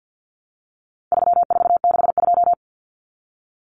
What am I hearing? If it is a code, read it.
Morse code, heard as a keyed tone, about 50 wpm, 700 Hz. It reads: 3462
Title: Common Cut Numbers - 4-Digit 50wpm